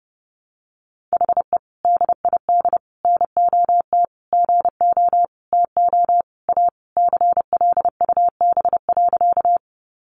Morse code, heard as a keyed tone, about 30 wpm, 700 Hz. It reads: HE DID NOT GO TO A CLUB.
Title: Sentences from Top 1200 Words 30wpm